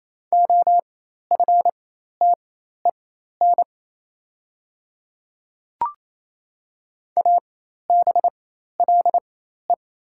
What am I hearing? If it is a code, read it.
Morse code, heard as a keyed tone, about 28 wpm, spaced as for 13 wpm, 690 Hz. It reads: OFTEN ABLE
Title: 4x Character Spacing - Top 500 Words - 28wpm